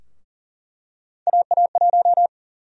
Morse code, heard as a keyed tone, about 40 wpm, 700 Hz. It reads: AA1
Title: USA Call Sign Prefixes - 40wpm